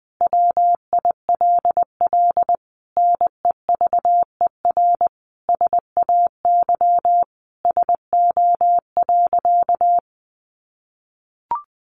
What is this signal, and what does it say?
Morse code, heard as a keyed tone, about 20 wpm, 700 Hz. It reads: WILL NEVER SAY SO.
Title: Sentences from Top 1400 Words 20wpm